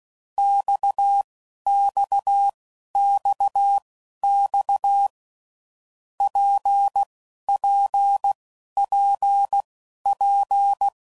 Morse code, heard as a keyed tone, about 16 wpm, spaced as for 11 wpm, 800 Hz. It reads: XXXX PPPP
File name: Groups / Basic Letter Groups / RKPX